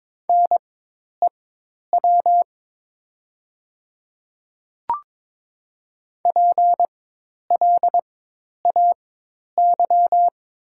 Morse code, heard as a keyed tone, about 22 wpm, spaced as for 10 wpm, 700 Hz. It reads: NEW PLAY
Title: 4x Character Spacing - Top 200 Words - 22wpm